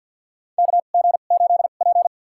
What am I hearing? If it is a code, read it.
Morse code, heard as a keyed tone, about 53 wpm, spaced as for 37 wpm, 700 Hz. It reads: KG8P
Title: USA Call Signs - 2x Character Spacing - 50wpm